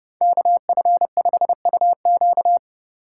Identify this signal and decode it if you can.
Morse code, heard as a keyed tone, about 27 wpm, 700 Hz. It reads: KF5UQ